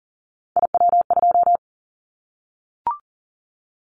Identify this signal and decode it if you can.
Morse code, heard as a keyed tone, about 40 wpm, 700 Hz. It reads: IW2
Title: Callsign Prefixes for Top 25 DXCC - 40wpm